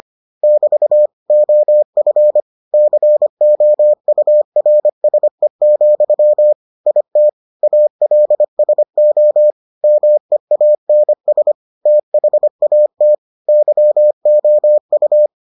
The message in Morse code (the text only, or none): <BT> OF COURSE, IT ALSO MEANS THAT YOU